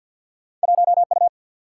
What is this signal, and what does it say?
Morse code, heard as a keyed tone, about 51 wpm, 700 Hz. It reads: 1 U
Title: Single-Letter Single-Number 50wpm